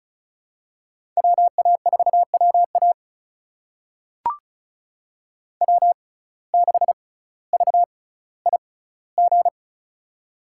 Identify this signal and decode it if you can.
Morse code, heard as a keyed tone, about 36 wpm, spaced as for 18 wpm, 700 Hz. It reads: WA4WA W6VIG
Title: USA Call Signs - 5 Characters - 6x Character Spacing - 35wpm